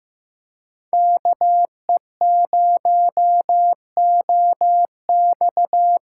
Morse code, heard as a keyed tone, about 15 wpm, 700 Hz. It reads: KE0OX
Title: USA Call Signs - 6 Characters - 6x Character Spacing - 15wpm